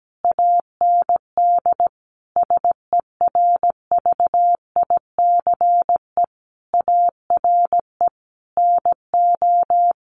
Morse code, heard as a keyed tone, about 17 wpm, 700 Hz. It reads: AND SERVICE ARE NO
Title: Sentences from Top 1200 Words 17wpm